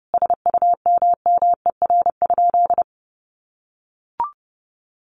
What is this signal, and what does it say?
Morse code, heard as a keyed tone, about 30 wpm, 700 Hz. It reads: SUMMER?
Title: Sentences from Top 1200 Words 30wpm